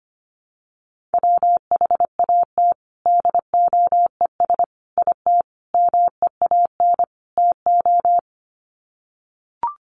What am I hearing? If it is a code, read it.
Morse code, heard as a keyed tone, about 25 wpm, 700 Hz. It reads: WHAT DOES IT MEAN TO
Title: Sets of 5 Words Encore2 25wpm